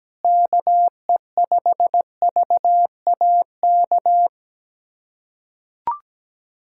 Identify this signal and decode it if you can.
Morse code, heard as a keyed tone, about 17 wpm, 700 Hz. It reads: KE5VAK
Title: USA Callsign DE Callsign 17wpm